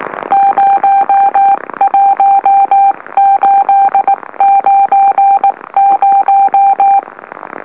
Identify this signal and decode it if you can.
Morse code, heard as a keyed tone, about 19 wpm, spaced as for 16 wpm, 790 Hz. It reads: 01890
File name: M04- -2345utc-LOLO